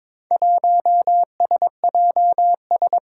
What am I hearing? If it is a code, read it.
Morse code, heard as a keyed tone, about 22 wpm, 700 Hz. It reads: 1SJS
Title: Four Challenging Letter-Number Combinations 22wpm